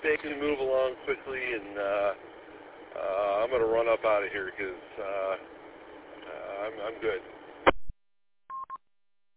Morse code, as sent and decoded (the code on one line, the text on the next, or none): none